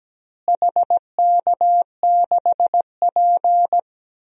Morse code, heard as a keyed tone, about 17 wpm, 700 Hz. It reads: HK6P
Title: DXCC-100 Call Signs - 4 Characters - Repeated 3x - 17wpm